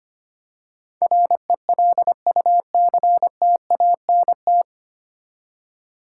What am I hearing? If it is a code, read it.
Morse code, heard as a keyed tone, about 25 wpm, 700 Hz. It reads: RELUCTANT
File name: practice / 25wpm / 9 Letter Words - First 3 Letters Given - 25wpm